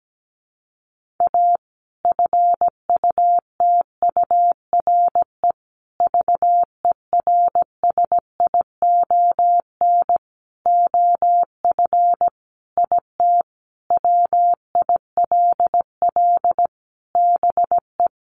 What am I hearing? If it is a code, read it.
Morse code, heard as a keyed tone, about 17 wpm, 700 Hz. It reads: A FUTURE VERSION OF IT WILL BE
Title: Sentences from Top 1400 Words 17wpm